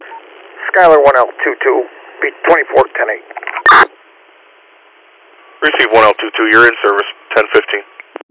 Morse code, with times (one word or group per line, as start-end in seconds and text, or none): none